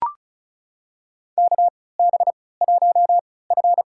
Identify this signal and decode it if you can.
Morse code, heard as a keyed tone, about 34 wpm, spaced as for 20 wpm, 700 Hz. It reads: KB1F
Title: USA Call Signs - 4 Characters - 3x Character Spacing - 35wpm